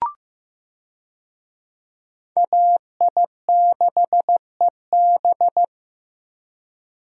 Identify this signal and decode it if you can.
Morse code, heard as a keyed tone, about 15 wpm, 700 Hz. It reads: AI6EB
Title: USA Call Signs - 5 Characters - Repeated 3x - 15wpm